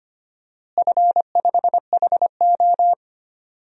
Morse code, heard as a keyed tone, about 25 wpm, 700 Hz. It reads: F5HO
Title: DXCC-100 Call Signs - 4 Characters - Repeated 2x - 25wpm